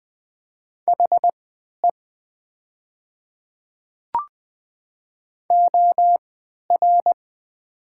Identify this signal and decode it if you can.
Morse code, heard as a keyed tone, about 20 wpm, spaced as for 11 wpm, 700 Hz. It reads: HE OR
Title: 3x Character Spacing - Top 200 Words - 20wpm